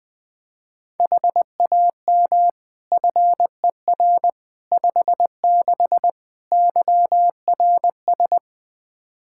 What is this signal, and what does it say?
Morse code, heard as a keyed tone, about 21 wpm, 700 Hz. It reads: HAM FER 56 YRS
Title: Practice QSO - Exchange 3 - More Info - 20wpm